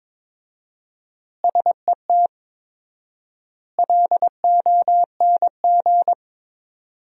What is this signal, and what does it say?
Morse code, heard as a keyed tone, about 22 wpm, 700 Hz. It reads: SET LONG
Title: 4x Word-Spacing - Sentences from Top 300 Words - 22wpm